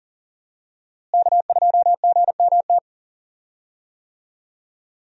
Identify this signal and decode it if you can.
Morse code, heard as a keyed tone, about 40 wpm, 700 Hz. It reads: K2GMT
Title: USA Call Signs - 5 Characters - Repeated 3x - 40wpm